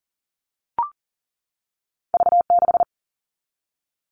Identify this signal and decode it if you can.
Morse code, heard as a keyed tone, about 38 wpm, 700 Hz. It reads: V6
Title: All Cut Numbers - 2-Digit 40wpm